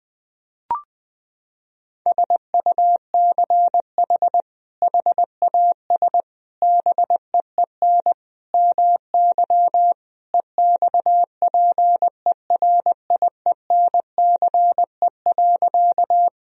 Morse code, heard as a keyed tone, about 21 wpm, 700 Hz. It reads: SUCH HAS BEEN MY EXPERIENCE.
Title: Sentences from Top 1000 Words - Review Part 3 - 20wpm